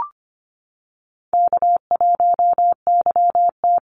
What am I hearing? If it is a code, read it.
Morse code, heard as a keyed tone, about 25 wpm, 700 Hz. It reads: K1YT
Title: Easy Call Signs - 25wpm